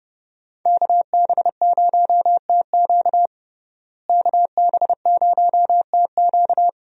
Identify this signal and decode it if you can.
Morse code, heard as a keyed tone, about 31 wpm, 700 Hz. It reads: KB0TQ KB0TQ
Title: USA Call Signs - 5 Characters - Repeated 3x - 30wpm